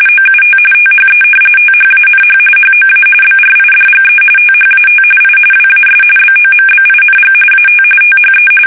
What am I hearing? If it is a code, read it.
RTTY, about 45.45 baud. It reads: OUT THE THANH HOA BRIDGE THAT I WAS BOMBING AT THE
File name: fsk850 / battleshipnj